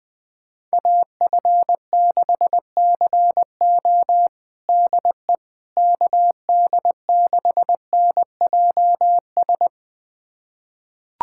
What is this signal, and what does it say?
Morse code, heard as a keyed tone, about 19 wpm, 700 Hz. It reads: AF6CO DE KD6NJS
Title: USA Callsign DE Callsign 20wpm